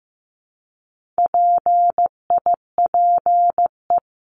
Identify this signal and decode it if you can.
Morse code, heard as a keyed tone, about 15 wpm, 700 Hz. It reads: PIPE
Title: Top 1251-1500 Words 15wpm